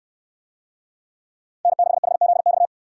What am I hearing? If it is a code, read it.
Morse code, heard as a keyed tone, about 65 wpm, 700 Hz. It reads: N6HZX